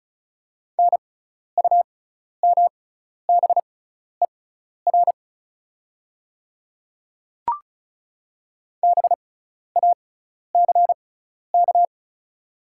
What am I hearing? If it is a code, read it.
Morse code, heard as a keyed tone, about 36 wpm, spaced as for 12 wpm, 690 Hz. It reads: NUMBER BACK
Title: 6x Character Spacing - Top 200 Words - 35wpm